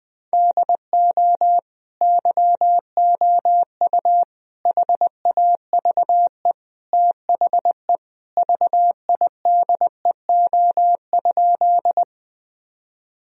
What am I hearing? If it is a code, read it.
Morse code, heard as a keyed tone, about 21 wpm, 700 Hz. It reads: DO YOU HAVE THE VIDEO?
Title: Sentences from Top 1000 Words - Review Part 4 - 20wpm